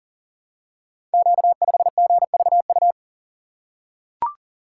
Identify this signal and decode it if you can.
Morse code, heard as a keyed tone, about 41 wpm, 690 Hz. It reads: Q5GVU
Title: Code Groups - 5 Letter:Numbers - 40wpm